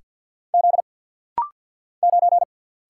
Morse code, heard as a keyed tone, about 50 wpm, 700 Hz. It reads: Z 9
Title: Single Letter-Number Alt - Rapid-Fire 50wpm